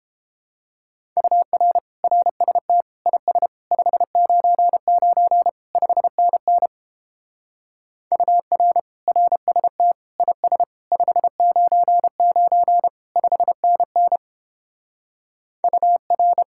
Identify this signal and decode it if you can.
Morse code, heard as a keyed tone, about 31 wpm, 700 Hz. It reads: UR RST IS 599 5NN UR RST IS 599 5NN UR